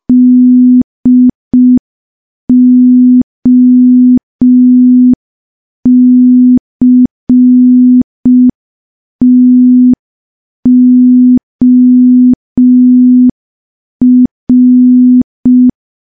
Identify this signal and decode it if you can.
Morse code, heard as a keyed tone, about 5 wpm, 250 Hz. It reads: DOCTOR
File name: morse / morse-training-05WPM